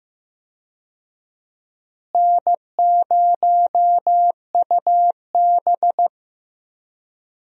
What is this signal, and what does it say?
Morse code, heard as a keyed tone, about 15 wpm, 700 Hz. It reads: N0UB
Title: USA Call Signs - 4 Characters - Repeated 2x - 15wpm